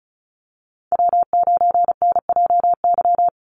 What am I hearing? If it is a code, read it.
Morse code, heard as a keyed tone, about 34 wpm, 700 Hz. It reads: W9NJY
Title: Call Signs - 35wpm